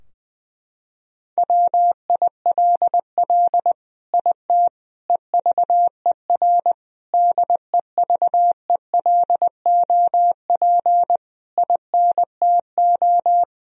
Morse code, heard as a keyed tone, about 20 wpm, 700 Hz. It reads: WILL IT EVER DEVELOP INTO
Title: Sentences from Top 1000 Words - Review Part 2 - 20wpm